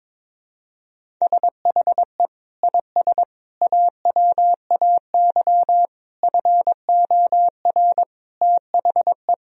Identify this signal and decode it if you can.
Morse code, heard as a keyed tone, about 22 wpm, 700 Hz. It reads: SHE IS AWAY FOR THE
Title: Sentences from Top 1000 Words - Review Part 2 - 22wpm